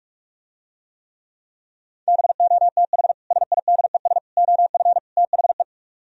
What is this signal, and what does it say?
Morse code, heard as a keyed tone, about 49 wpm, 700 Hz. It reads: BOTH SIDES OF THE